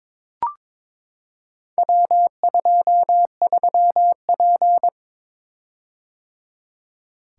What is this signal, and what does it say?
Morse code, heard as a keyed tone, about 22 wpm, 700 Hz. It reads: W23P
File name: practice / 22wpm / Four Letter-Number Combinations - 22wpm